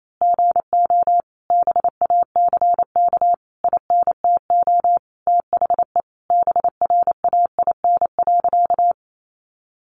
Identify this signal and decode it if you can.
Morse code, heard as a keyed tone, about 28 wpm, 700 Hz. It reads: GO BACK INTO THE BRAIN.
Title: A Sound Of Thunder 28wpm